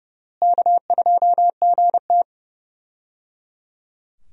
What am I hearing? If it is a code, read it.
Morse code, heard as a keyed tone, about 32 wpm, 700 Hz. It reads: K2GT
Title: USA Call Signs - 4 Characters - Repeated 3x - 30wpm